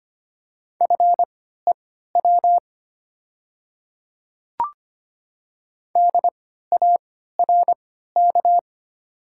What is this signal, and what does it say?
Morse code, heard as a keyed tone, about 25 wpm, spaced as for 14 wpm, 690 Hz. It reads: FEW DARK
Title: 3x Character Spacing - Top 500 Words - 25wpm